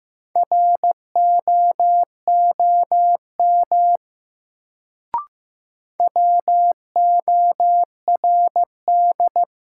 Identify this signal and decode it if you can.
Morse code, heard as a keyed tone, about 15 wpm, 700 Hz. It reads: ROOM WORD
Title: Sentences from Top 300 Words - New Word Drill - 15wpm